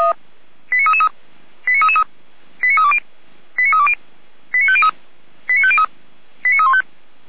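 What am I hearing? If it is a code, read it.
Five-tone selective call (ZVEI), heard as quick runs of five tones, tones of about 70 ms. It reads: 89333 89333 89320 89320 89503 89503 89315
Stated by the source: ZVEI five-tone selective call received with an SDR, Feuerwehr Orlamünde.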